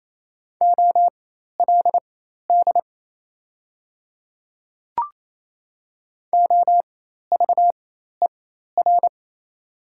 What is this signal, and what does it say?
Morse code, heard as a keyed tone, about 28 wpm, spaced as for 13 wpm, 690 Hz. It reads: OLD OVER
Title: 4x Character Spacing - Top 500 Words - 28wpm